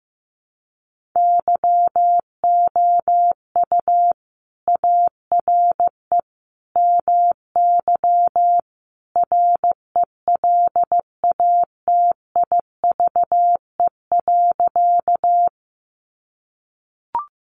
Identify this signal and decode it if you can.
Morse code, heard as a keyed tone, about 15 wpm, 700 Hz. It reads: YOU ARE MY RELATIVE.
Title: Sentences from Top 1400 Words 15wpm